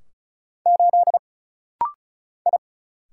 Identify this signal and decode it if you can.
Morse code, heard as a keyed tone, about 35 wpm, 700 Hz. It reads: 8 I
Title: Single Letter-Number - Rapid-Fire 35wpm